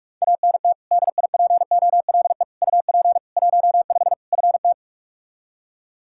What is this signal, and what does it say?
Morse code, heard as a keyed tone, about 46 wpm, 700 Hz. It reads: ANT DIPOLE UP 15 FT
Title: Practice QSO - Exchange 2 - Info - 45wpm